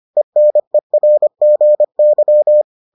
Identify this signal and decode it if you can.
Morse code, heard as a keyed tone, about 25 wpm, 600 Hz. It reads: ENERGY